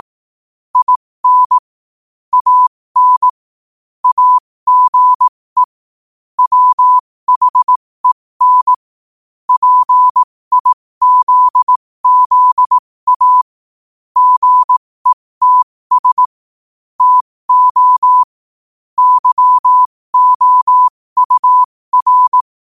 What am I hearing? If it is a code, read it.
Morse code, heard as a keyed tone, about 18 wpm, spaced as for 15 wpm, 1000 Hz. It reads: IN AN AGE WHEN PIZZA GETS TO YOUR
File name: qotdfiles / 15wpm